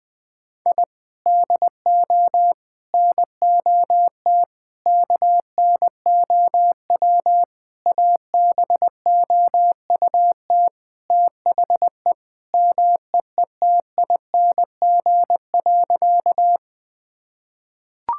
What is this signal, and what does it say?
Morse code, heard as a keyed tone, about 20 wpm, 700 Hz. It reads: I DO NOT KNOW ABOUT THE MEETING.
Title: Sentences with 5-Word Phrases - 20wpm